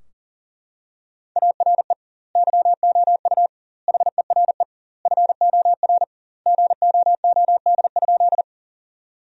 Morse code, heard as a keyed tone, about 40 wpm, 700 Hz. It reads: ARE YOU HERE FOR GOOD?
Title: 2x Word-Spacing - Sentences from Top 200 Words - 40wpm